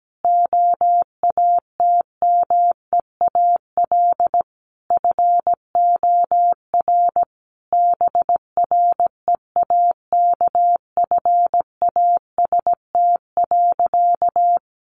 Morse code, heard as a keyed tone, about 17 wpm, 700 Hz. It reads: OATMEAL FOR BREAKFAST.
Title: Easy Phrases 17wpm